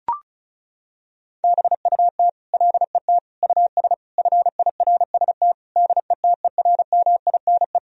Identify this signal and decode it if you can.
Morse code, heard as a keyed tone, about 34 wpm, 700 Hz. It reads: BUT LET US FIRST DETERMINE
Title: Sentences from Top 1000 Words - Review Part 4 - 35wpm